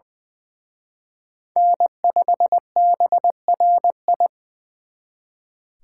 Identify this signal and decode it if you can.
Morse code, heard as a keyed tone, about 23 wpm, 700 Hz. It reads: N5BRI
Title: Speed Racing - USA 1x3 Call Signs 20wpm